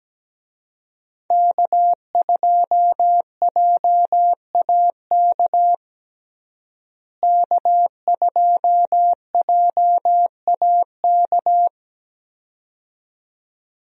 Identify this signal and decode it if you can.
Morse code, heard as a keyed tone, about 17 wpm, 700 Hz. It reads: K2JAK K2JAK
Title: USA Call Signs - 5 Characters - Repeated 2x - 17wpm